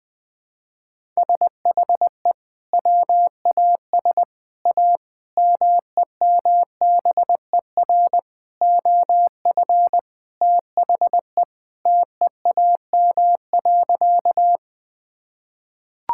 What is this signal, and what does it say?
Morse code, heard as a keyed tone, about 20 wpm, 700 Hz. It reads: SHE WAS A MEMBER OF THE TEAM.
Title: Sentences with 5-Word Phrases - Spoken Beginning - 20wpm